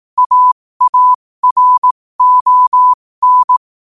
Morse code, heard as a keyed tone, about 18 wpm, spaced as for 15 wpm, 1000 Hz. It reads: AARON
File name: qotdfiles / 15wpm